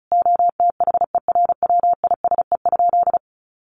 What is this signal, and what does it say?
Morse code, heard as a keyed tone, about 34 wpm, 700 Hz. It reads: OTHERWISE?
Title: Sentences from Top 1200 Words 35wpm